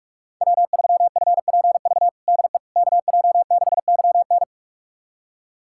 Morse code, heard as a keyed tone, about 46 wpm, 700 Hz. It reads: W3FPV DE KJ6YN
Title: USA Callsign DE Callsign - 5 Characters - 45wpm